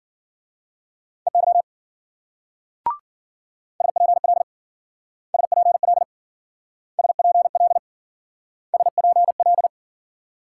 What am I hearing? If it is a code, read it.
Morse code, heard as a keyed tone, about 50 wpm, 700 Hz. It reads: EY SPL SPL SPL SPL
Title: Speed-Racing Multi-letter Phonemes - 40wpm